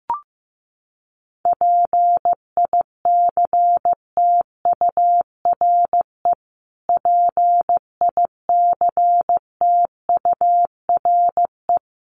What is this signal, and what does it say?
Morse code, heard as a keyed tone, about 14 wpm, 700 Hz. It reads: PICTURE PICTURE
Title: Top 500 Words Repeated 15wpm